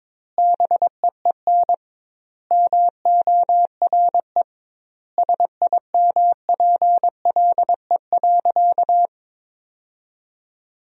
Morse code, heard as a keyed tone, about 22 wpm, 700 Hz. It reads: BEEN MORE SIMPLE.
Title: 2x Word-Spacing - Sentences from Top 400 Words - 22wpm